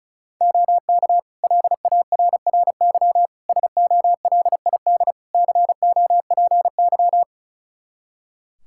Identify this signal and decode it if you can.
Morse code, heard as a keyed tone, about 34 wpm, 700 Hz. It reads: OK LARRY SOLID COPY
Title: Practice QSO - Exchange 3 - More Info - 35wpm